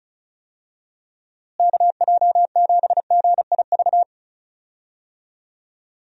Morse code, heard as a keyed tone, about 36 wpm, 700 Hz. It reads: KJ7GIV
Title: USA Call Signs - 6 Characters - Repeated 2x - 35wpm